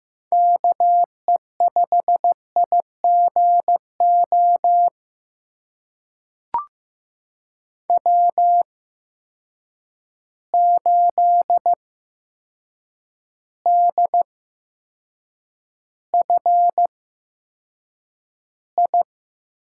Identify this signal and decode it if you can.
Morse code, heard as a keyed tone, about 15 wpm, spaced as for 7 wpm, 700 Hz. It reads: KE5IGO W8DFI
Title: USA Call Signs - 8x Character Spacing - 15wpm